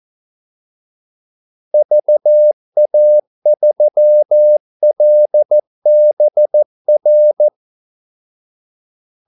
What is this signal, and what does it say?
Morse code, heard as a keyed tone, about 14 wpm, 600 Hz. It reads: VA3LBR
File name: Callsigns-14wpm